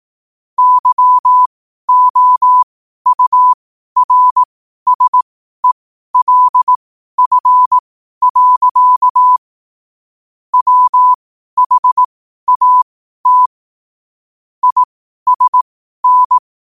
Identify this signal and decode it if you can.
Morse code, heard as a keyed tone, about 18 wpm, spaced as for 12 wpm, 1000 Hz. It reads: YOURSELF. WHAT ISN
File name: qotdfiles / 12wpm